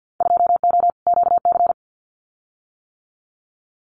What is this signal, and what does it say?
Morse code, heard as a keyed tone, about 50 wpm, 700 Hz. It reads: 2O QC
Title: Winter Field Day Exchanges 50wpm